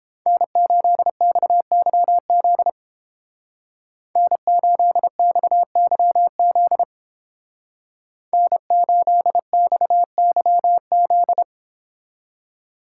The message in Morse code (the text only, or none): N8XYZ N8XYZ N8XYZ